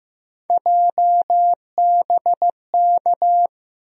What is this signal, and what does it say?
Morse code, heard as a keyed tone, about 15 wpm, 700 Hz. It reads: JBK
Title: USA Call Sign Postfixes - 3 Letters 15wpm